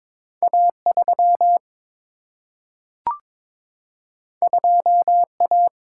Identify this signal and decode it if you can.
Morse code, heard as a keyed tone, about 22 wpm, 700 Hz. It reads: A3 2A
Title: Single-Letter Single-Number 22wpm